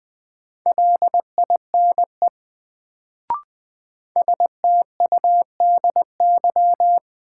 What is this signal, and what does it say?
Morse code, heard as a keyed tone, about 20 wpm, 700 Hz. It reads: LINE STUDY
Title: Sets of 5 Words - New Word Drill - 20wpm